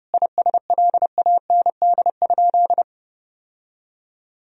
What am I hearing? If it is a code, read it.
Morse code, heard as a keyed tone, about 29 wpm, 700 Hz. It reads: ISLAND?
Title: Sentences from Top 1000 Words - Review Part 3 - 30wpm